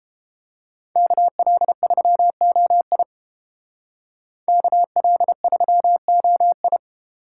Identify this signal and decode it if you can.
Morse code, heard as a keyed tone, about 30 wpm, 700 Hz. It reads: KL3OI KL3OI